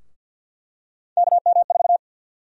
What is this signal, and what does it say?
Morse code, heard as a keyed tone, about 51 wpm, 700 Hz. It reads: KM4
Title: USA Call Sign Prefixes - Repeated 2x - 50wpm